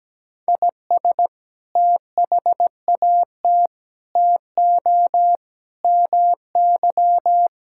Morse code, heard as a keyed tone, about 17 wpm, 700 Hz. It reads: IS THAT TO MY
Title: Sentences from Top 1000 Words - Review Part 6 - 17wpm